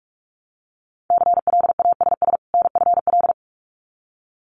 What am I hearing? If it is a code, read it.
Morse code, heard as a keyed tone, about 46 wpm, 700 Hz. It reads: CLASS NFL
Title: Winter Field Day Exchanges 45wpm